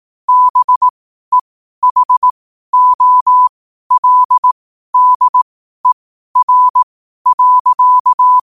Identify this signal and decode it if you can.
Morse code, heard as a keyed tone, about 18 wpm, spaced as for 12 wpm, 1000 Hz. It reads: BEHOLDER.
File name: qotdfiles / 12wpm